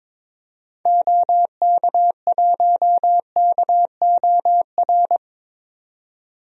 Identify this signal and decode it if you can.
Morse code, heard as a keyed tone, about 22 wpm, 700 Hz. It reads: OK1KOR
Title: DXCC-100 Call Signs - 6 Characters - 22wpm